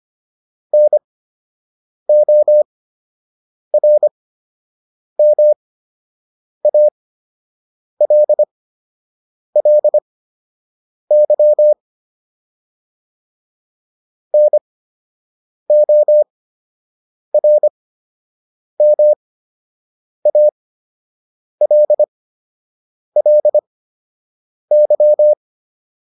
Morse code, heard as a keyed tone, about 25 wpm, spaced as for 7 wpm, 600 Hz. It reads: NORMALLY NORMALLY